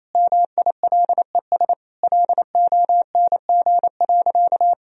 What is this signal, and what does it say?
Morse code, heard as a keyed tone, about 28 wpm, 700 Hz. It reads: MILES LONG.
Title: Sentences from Top 1000 Words - Review Part 3 - 28wpm